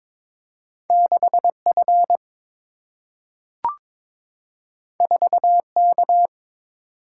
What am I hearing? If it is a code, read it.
Morse code, heard as a keyed tone, about 22 wpm, 700 Hz. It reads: 6 F 4 K
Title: Single-Letter Single-Number 22wpm